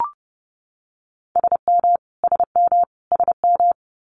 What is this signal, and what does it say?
Morse code, heard as a keyed tone, about 30 wpm, 700 Hz. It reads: SM SM SM
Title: Multi-letter Phonemes Repeated 3x 30wpm